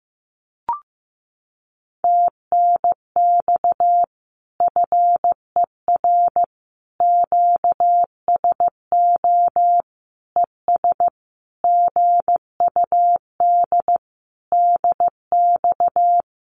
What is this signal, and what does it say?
Morse code, heard as a keyed tone, about 15 wpm, 700 Hz. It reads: TNX FER QSO ES GUD DX
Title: Common QSO Phrases Encore 15wpm